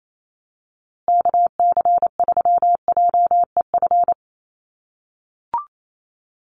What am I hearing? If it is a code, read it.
Morse code, heard as a keyed tone, about 28 wpm, 700 Hz. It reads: KC3JEF
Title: USA Call Signs Repeated 3x 28wpm